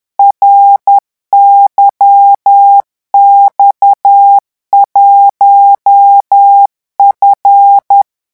Morse code, heard as a keyed tone, about 10 wpm, 800 Hz. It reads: RYX1F
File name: morse mp3 / revision19-vitesse 10 mots